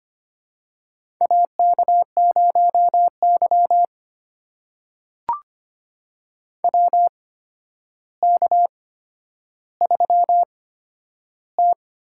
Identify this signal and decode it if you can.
Morse code, heard as a keyed tone, about 25 wpm, spaced as for 11 wpm, 700 Hz. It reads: AK0Y WK3T
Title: USA Call Signs - 4 Characters - 8x Character Spacing - 25wpm